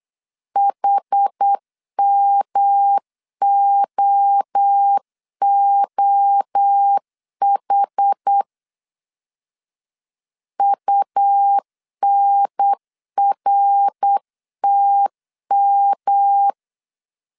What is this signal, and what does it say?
Morse code, heard as a keyed tone, about 8 wpm, 800 Hz. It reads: HMOOH UNRTM